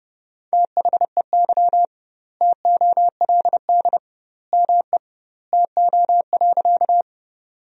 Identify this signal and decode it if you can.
Morse code, heard as a keyed tone, about 32 wpm, 700 Hz. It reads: THEY TOLD ME TO.
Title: 2x Word-Spacing - Sentences from Top 400 Words - 30wpm